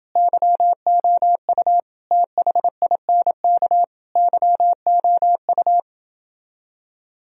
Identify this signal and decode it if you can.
Morse code, heard as a keyed tone, about 28 wpm, 700 Hz. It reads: YOU THINK YOU